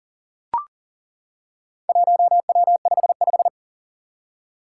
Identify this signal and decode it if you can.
Morse code, heard as a keyed tone, about 42 wpm, 700 Hz. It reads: 1W55
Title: Four Challenging Letter-Number Combinations 40wpm